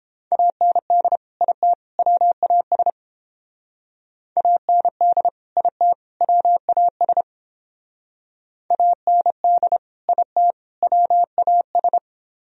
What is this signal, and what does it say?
Morse code, heard as a keyed tone, about 30 wpm, 700 Hz. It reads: AND IT WAS AND IT WAS AND IT WAS